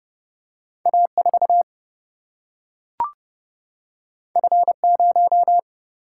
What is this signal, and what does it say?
Morse code, heard as a keyed tone, about 29 wpm, 700 Hz. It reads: A4 F0
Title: Single-Letter Single-Number 30wpm